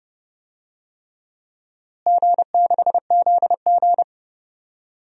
Morse code, heard as a keyed tone, about 32 wpm, 700 Hz. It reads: G6ZG
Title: DXCC-100 Call Signs - 4 Characters - Repeated 3x - 30wpm